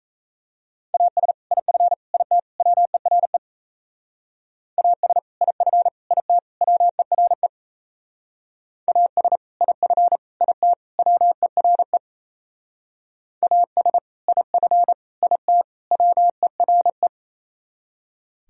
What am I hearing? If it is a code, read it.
Morse code, heard as a keyed tone, about 34 wpm, 700 Hz. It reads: AS IF IT WERE AS IF IT WERE AS IF IT WERE AS IF IT WERE
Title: Speed-Racing - Sets of 4 Words 28wpm